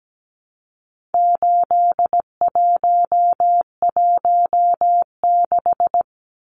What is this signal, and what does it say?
Morse code, heard as a keyed tone, about 17 wpm, 700 Hz. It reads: 8116
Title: Numbers - 4-Digit 17wpm